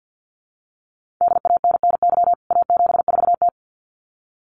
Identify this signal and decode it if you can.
Morse code, heard as a keyed tone, about 50 wpm, 700 Hz. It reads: DANNY A74T
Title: CWT Exchange 50wpm